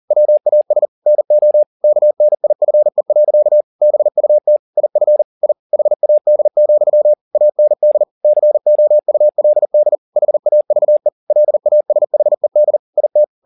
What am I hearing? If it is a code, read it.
Morse code, heard as a keyed tone, about 40 wpm, 600 Hz. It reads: WAS NO KNIFE. BUT IF I HAD, AND COULD HAVE LASHED IT
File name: the old man and the sea 2 40wpm